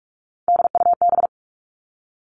Morse code, heard as a keyed tone, about 46 wpm, 700 Hz. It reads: DUB
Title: Three Challenging Letter-Number Combinations 45wpm